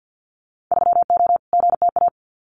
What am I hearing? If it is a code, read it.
Morse code, heard as a keyed tone, about 50 wpm, 700 Hz. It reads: 3O GTA
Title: Winter Field Day Exchanges Encore 50wpm